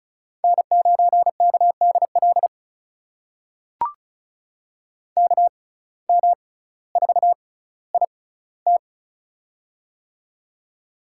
Morse code, heard as a keyed tone, about 36 wpm, spaced as for 18 wpm, 700 Hz. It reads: N9KDL KM4IT
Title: USA Call Signs - 5 Characters - 6x Character Spacing - 35wpm